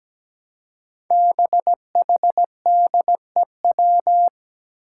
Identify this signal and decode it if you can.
Morse code, heard as a keyed tone, about 17 wpm, 700 Hz. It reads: BHDEW
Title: Five Challenging Letter-Number Combinations 17wpm